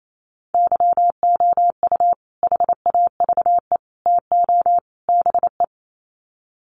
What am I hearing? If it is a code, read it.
Morse code, heard as a keyed tone, about 28 wpm, 700 Hz. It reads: YOU HAVE TO BE
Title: Sets of 4 Words 28wpm